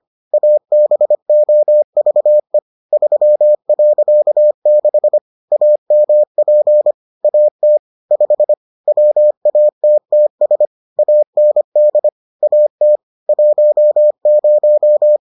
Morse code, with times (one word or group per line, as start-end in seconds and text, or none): 0.3-2.6 ABOVE
2.9-5.2 3.6
5.5-6.9 AMP
7.2-7.8 AT
8.1-8.5 5
8.9-10.7 WATTS
11.0-12.1 AND
12.4-13.0 AT
13.3-15.2 10